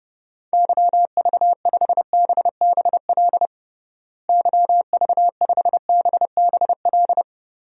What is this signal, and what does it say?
Morse code, heard as a keyed tone, about 30 wpm, 700 Hz. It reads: YV5BBL YV5BBL
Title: DXCC-100 Call Signs - 6 Characters - Repeated 3x - 30wpm